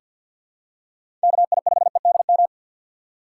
Morse code, heard as a keyed tone, about 46 wpm, 700 Hz. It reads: KI5EDM
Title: USA Call Signs - 6 Characters - 4x Character Spacing - 50wpm